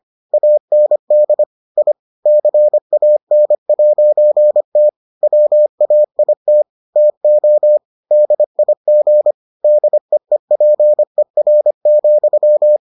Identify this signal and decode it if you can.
Morse code, heard as a keyed tone, about 25 wpm, 600 Hz. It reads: AND I CAN'T WAIT TO DIG DEEPER,